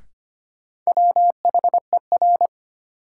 Morse code, heard as a keyed tone, about 25 wpm, 700 Hz. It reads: WHER
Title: Sentences from Top 1500 Words - New Word Drill - 25wpm